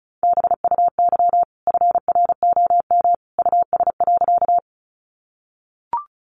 Morse code, heard as a keyed tone, about 34 wpm, 700 Hz. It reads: BUY FROM US.
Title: Sentences from Top 700 Words 35wpm